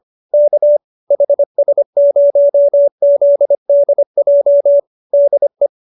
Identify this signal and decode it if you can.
Morse code, heard as a keyed tone, about 25 wpm, 570 Hz. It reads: K HS0ZDJ DE